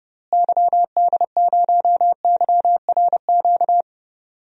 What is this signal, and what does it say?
Morse code, heard as a keyed tone, about 30 wpm, 700 Hz. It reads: YD0YRQ
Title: DXCC-100 Call Signs - 6 Characters - Repeated 3x - 30wpm